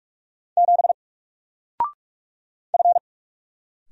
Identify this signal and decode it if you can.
Morse code, heard as a keyed tone, about 46 wpm, 700 Hz. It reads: 7 F
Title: Single Character (Letter:Number:Punctuation:Prosign) Alt 45wpm